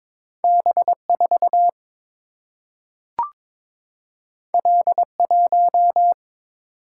Morse code, heard as a keyed tone, about 22 wpm, 700 Hz. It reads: B 4 L 1
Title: Single-Letter Single-Number 22wpm